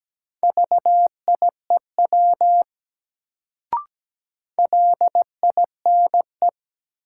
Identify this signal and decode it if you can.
Morse code, heard as a keyed tone, about 17 wpm, 700 Hz. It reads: VIEW LINE
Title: Sets of 5 Words - New Word Drill - 17wpm